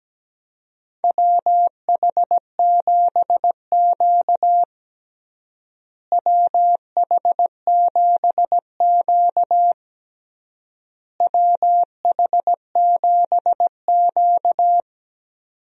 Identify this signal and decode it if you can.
Morse code, heard as a keyed tone, about 17 wpm, 700 Hz. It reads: WH7Q WH7Q WH7Q
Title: USA Call Signs - 4 Characters - Repeated 3x - 17wpm